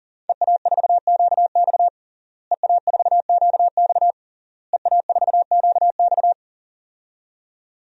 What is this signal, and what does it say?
Morse code, heard as a keyed tone, about 40 wpm, 700 Hz. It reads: EA4QX EA4QX EA4QX
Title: DXCC-100 Call Signs - 5 Characters - Repeated 3x - 40wpm